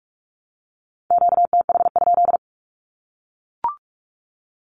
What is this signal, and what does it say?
Morse code, heard as a keyed tone, about 44 wpm, 700 Hz. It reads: QTH?
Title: Common QSO Phrases 45wpm